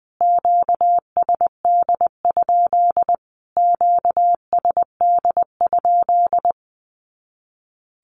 Morse code, heard as a keyed tone, about 20 wpm, 700 Hz. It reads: QSD? QSD?
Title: Q-Codes Repeated 3x 20wpm